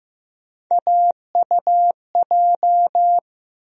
Morse code, heard as a keyed tone, about 15 wpm, 700 Hz. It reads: AUJ
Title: USA Call Sign Postfixes - 3 Letters 15wpm